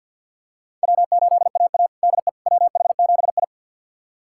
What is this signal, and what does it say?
Morse code, heard as a keyed tone, about 51 wpm, 700 Hz. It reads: W8AA DE WH7I
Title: USA Callsign DE Callsign - 4 Characters - 50wpm